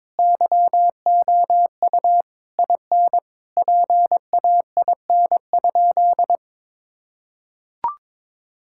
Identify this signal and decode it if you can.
Morse code, heard as a keyed tone, about 22 wpm, 700 Hz. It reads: YOU IN PAIN?
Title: Sentences from Top 1600 Words 22wpm